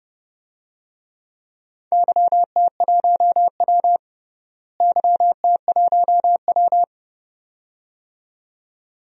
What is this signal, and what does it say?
Morse code, heard as a keyed tone, about 30 wpm, 700 Hz. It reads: YT1W YT1W
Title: DXCC-100 Call Signs - 4 Characters - Repeated 2x - 30wpm